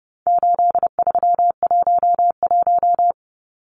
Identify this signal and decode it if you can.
Morse code, heard as a keyed tone, about 30 wpm, 700 Hz. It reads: 8311
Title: Numbers - 4-Digit 30wpm